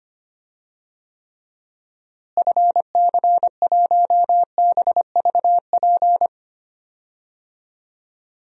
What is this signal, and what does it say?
Morse code, heard as a keyed tone, about 25 wpm, 700 Hz. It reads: FC1BVP
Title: DXCC-100 Call Signs - 6 Characters - Repeated 3x - 25wpm